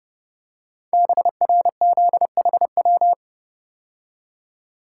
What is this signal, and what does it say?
Morse code, heard as a keyed tone, about 28 wpm, 700 Hz. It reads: BRZHW
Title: Code Groups - 5 Letters - 30wpm